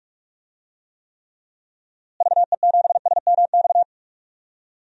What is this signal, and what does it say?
Morse code, heard as a keyed tone, about 43 wpm, 700 Hz. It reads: VE7SMX
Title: DXCC-100 Call Signs - 6 Characters - Repeated 2x - 45wpm